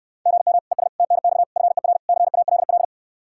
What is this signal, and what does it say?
Morse code, heard as a keyed tone, about 67 wpm, 700 Hz. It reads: GM ES TNX FER CALL